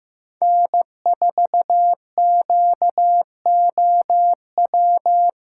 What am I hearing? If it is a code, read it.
Morse code, heard as a keyed tone, about 16 wpm, 700 Hz. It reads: N4QOW
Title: USA Callsign DE Callsign - 5 Characters - 15wpm